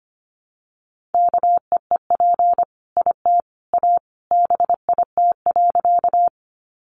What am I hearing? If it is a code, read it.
Morse code, heard as a keyed tone, about 25 wpm, 700 Hz. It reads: KEEP IT A BIT.
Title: Sentences from Top 900 Words 25wpm